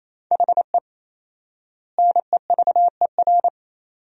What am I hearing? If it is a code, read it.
Morse code, heard as a keyed tone, about 28 wpm, 700 Hz. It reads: HE NEVER
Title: 4x Word-Spacing - Sentences from Top 400 Words - 28wpm